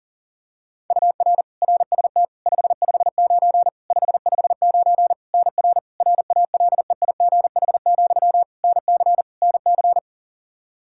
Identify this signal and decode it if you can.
Morse code, heard as a keyed tone, about 40 wpm, 700 Hz. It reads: UR RST 559 559 NR RALEIGH, NC NC
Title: K1USN SST QSO Practice 40wpm